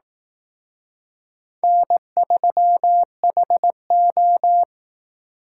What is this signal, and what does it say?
Morse code, heard as a keyed tone, about 18 wpm, 700 Hz. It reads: N3HO
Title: Speed Racing - USA 1x2 Call Signs 15wpm